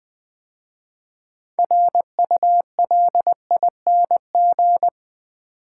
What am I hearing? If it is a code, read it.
Morse code, heard as a keyed tone, about 20 wpm, 700 Hz. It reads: RULING
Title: 6 Letter Words - First 3 Letters Given - 20wpm